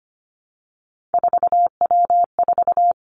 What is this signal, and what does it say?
Morse code, heard as a keyed tone, about 25 wpm, 700 Hz. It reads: 4W4
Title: Three Challenging Letter-Number Combinations 25wpm